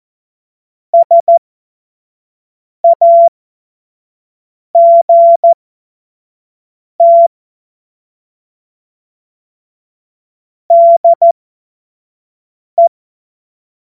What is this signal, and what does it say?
Morse code, heard as a keyed tone, about 14 wpm, spaced as for 5 wpm, 680 Hz. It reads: SAGT DE
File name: Morsekurs / Witze / Witz 01. Juni 2021, 25 F70